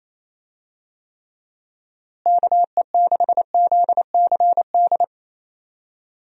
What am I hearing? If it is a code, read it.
Morse code, heard as a keyed tone, about 28 wpm, 700 Hz. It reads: KE6ZCD
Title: USA Call Signs - 6 Characters - Repeated 2x - 28wpm